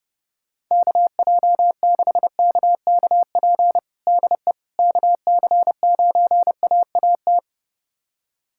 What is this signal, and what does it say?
Morse code, heard as a keyed tone, about 32 wpm, 700 Hz. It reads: KJ6KKP DE KC9AAT
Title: USA Callsign DE Callsign - 6 Characters - 30wpm